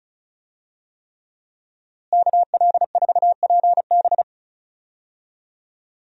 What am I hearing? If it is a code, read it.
Morse code, heard as a keyed tone, about 36 wpm, 700 Hz. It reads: KL4PB
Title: USA Call Signs - 5 Characters - Repeated 3x - 35wpm